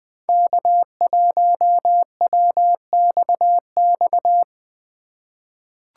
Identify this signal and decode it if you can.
Morse code, heard as a keyed tone, about 20 wpm, 700 Hz. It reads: K1WXX
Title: USA Call Signs - 5 Characters - 4x Character Spacing - 20wpm